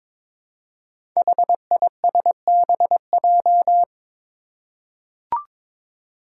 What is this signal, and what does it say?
Morse code, heard as a keyed tone, about 22 wpm, 700 Hz. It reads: HISBJ
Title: Five Challenging Letter-Number Combinations 22wpm